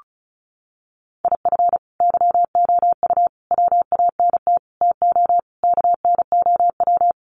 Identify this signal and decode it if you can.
Morse code, heard as a keyed tone, about 34 wpm, 700 Hz. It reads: IF YOU WANT TO KNOW
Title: Sets of 5 Words Encore2 35wpm